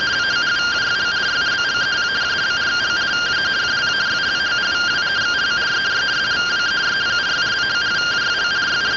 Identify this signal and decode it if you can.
Two-tone teleprinter signal, about 170 Hz shift.